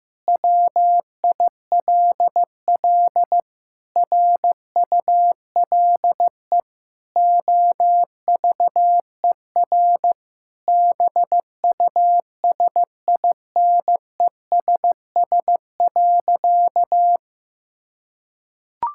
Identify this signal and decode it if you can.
Morse code, heard as a keyed tone, about 16 wpm, 700 Hz. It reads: WILL RULE OVER BUSINESS.
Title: Sentences from Top 1000 Words - Review Part 5 - 15wpm